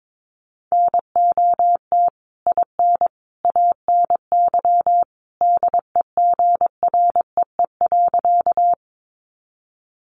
Morse code, heard as a keyed tone, about 22 wpm, 700 Hz. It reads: NOT IN ANY DEGREE.
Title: Sentences from Top 1500 Words 22wpm